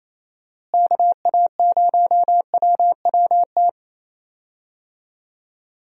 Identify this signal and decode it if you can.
Morse code, heard as a keyed tone, about 27 wpm, 700 Hz. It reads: KA0WWT
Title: USA Call Signs - 6 Characters - Repeated 3x - 28wpm